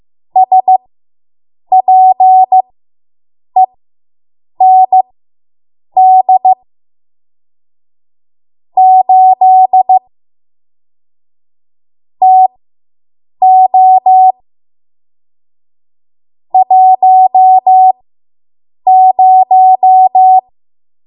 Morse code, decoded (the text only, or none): SPEND 8 TO 10